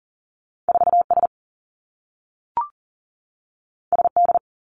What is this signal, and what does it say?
Morse code, heard as a keyed tone, about 39 wpm, 700 Hz. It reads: 4S SD
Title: Two Challenging Letter-Number Combinations 40wpm